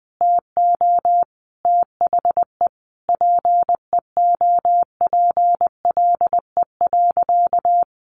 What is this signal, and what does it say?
Morse code, heard as a keyed tone, about 20 wpm, 700 Hz. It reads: TO THE PEOPLE.
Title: Sentences from Top 1500 Words 20wpm